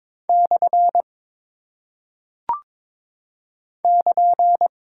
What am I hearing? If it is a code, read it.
Morse code, heard as a keyed tone, about 22 wpm, 700 Hz. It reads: / <KN>
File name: practice / 22wpm / Single Punctuation-Prosign - 22wpm